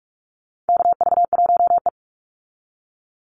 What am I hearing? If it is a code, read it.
Morse code, heard as a keyed tone, about 44 wpm, 700 Hz. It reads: KV1E
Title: Call Signs - 45wpm